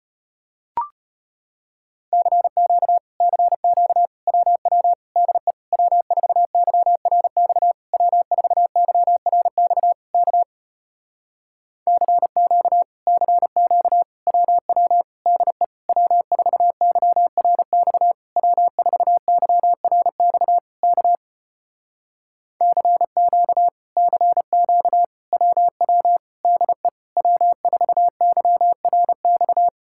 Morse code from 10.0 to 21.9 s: K CQ CQ WW DE W4YRX W4YRX K

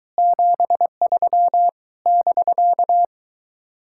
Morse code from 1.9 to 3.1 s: <BK>

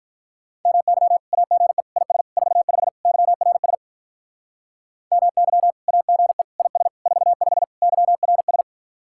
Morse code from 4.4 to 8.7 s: MY AGE IS 45 YRS

